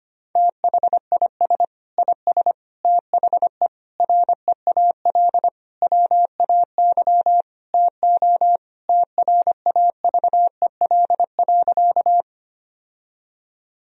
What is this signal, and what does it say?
Morse code, heard as a keyed tone, about 25 wpm, 700 Hz. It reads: THIS IS THE REAL WAY TO TRAVEL.
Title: Sentences from Top 1000 Words - Review Part 2 - 25wpm